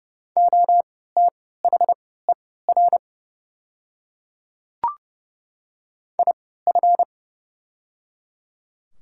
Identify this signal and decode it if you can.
Morse code, heard as a keyed tone, about 30 wpm, spaced as for 17 wpm, 700 Hz. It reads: OTHER IF
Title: 3x Character Spacing - Top 300 Words - 30wpm